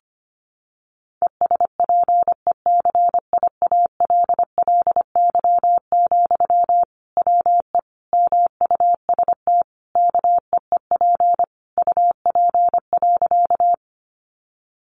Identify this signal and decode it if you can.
Morse code, heard as a keyed tone, about 25 wpm, 700 Hz. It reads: ESPECIALLY, WE MUST KEEP UP.
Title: Sentences from Top 700 Words 25wpm